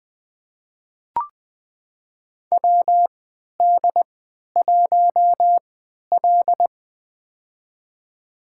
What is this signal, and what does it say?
Morse code, heard as a keyed tone, about 20 wpm, spaced as for 11 wpm, 700 Hz. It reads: WD1L
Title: USA Call Signs - 4 Characters - 3x Character Spacing - 20wpm